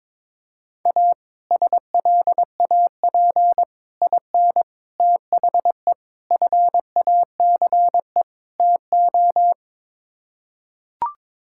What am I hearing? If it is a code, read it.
Morse code, heard as a keyed tone, about 22 wpm, 700 Hz. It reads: A SLAP IN THE FACE TO
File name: practice / 22wpm / Sets of 6 Words - 22wpm